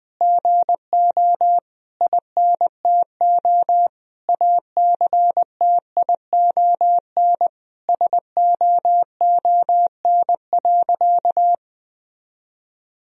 Morse code, heard as a keyed tone, about 20 wpm, 700 Hz. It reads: GO INTO ACTION SOON.
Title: Sentences from Top 1000 Words - Review Part 4 - 20wpm